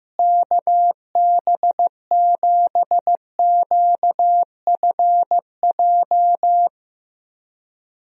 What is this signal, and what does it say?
Morse code, heard as a keyed tone, about 15 wpm, 700 Hz. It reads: KB7QFJ
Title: USA Call Signs - 6 Characters - Repeated 2x - 15wpm